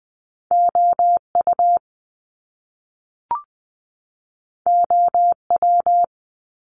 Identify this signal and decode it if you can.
Morse code, heard as a keyed tone, about 20 wpm, 700 Hz. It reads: O U O W
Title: Top Two Letter Combinations in Words 20wpm